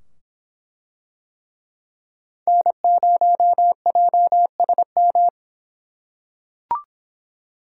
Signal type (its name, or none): Morse code